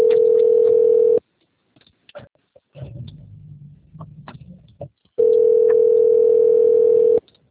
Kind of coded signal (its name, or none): telephone call progress tone